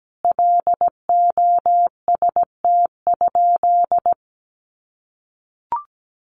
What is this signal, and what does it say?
Morse code, heard as a keyed tone, about 17 wpm, 700 Hz. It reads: LOST?
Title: Sentences from Top 1400 Words 17wpm